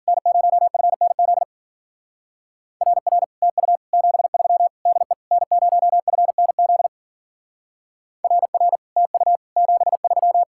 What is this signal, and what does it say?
Morse code, heard as a keyed tone, about 51 wpm, 700 Hz. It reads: N0FNZ RR TU 73 DE N0FNZ RR TU 73